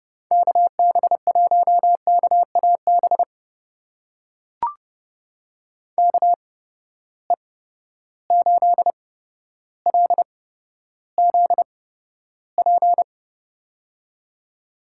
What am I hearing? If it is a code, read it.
Morse code, heard as a keyed tone, about 30 wpm, spaced as for 13 wpm, 700 Hz. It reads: KB1KAB KE8LZP
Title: USA Call Signs - 6 Characters - 8x Character Spacing - 30wpm